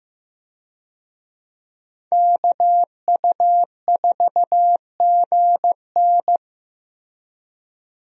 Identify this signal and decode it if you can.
Morse code, heard as a keyed tone, about 15 wpm, 700 Hz. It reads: KU4GN
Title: USA Call Signs - 5 Characters - Repeated 3x - 15wpm